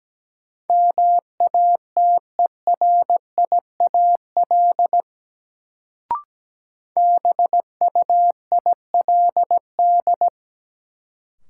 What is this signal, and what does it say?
Morse code, heard as a keyed tone, about 17 wpm, 700 Hz. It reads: MATERIAL BUILD
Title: Sentences from Top 1100 Words - New Word Drill Alt - 17wpm